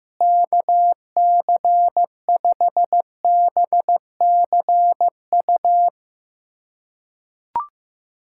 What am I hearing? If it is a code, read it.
Morse code, heard as a keyed tone, about 16 wpm, 700 Hz. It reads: KC5BCU
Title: USA Callsign DE Callsign - 6 Characters - 15wpm